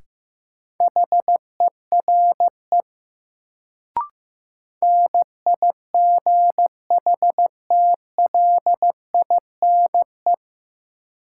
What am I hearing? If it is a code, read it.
Morse code, heard as a keyed tone, about 15 wpm, 700 Hz. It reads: HERE NIGHTLINE
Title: Sets of 5 Words - New Word Drill - 15wpm